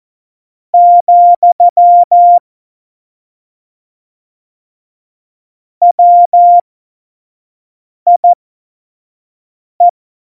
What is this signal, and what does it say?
Morse code, heard as a keyed tone, about 14 wpm, spaced as for 5 wpm, 700 Hz. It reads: , WIE
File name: Morsekurs / Witze / Witz 29. Oktober 2019, 25 F70